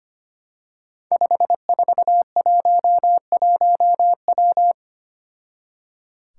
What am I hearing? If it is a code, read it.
Morse code, heard as a keyed tone, about 25 wpm, 700 Hz. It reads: 5411W
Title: Five Challenging Letter-Number Combinations 25wpm